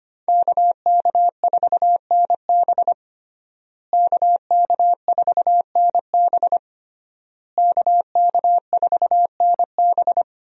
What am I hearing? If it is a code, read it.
Morse code, heard as a keyed tone, about 25 wpm, 700 Hz. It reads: KK4NB KK4NB KK4NB
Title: USA Call Signs - 5 Characters - Repeated 3x - 25wpm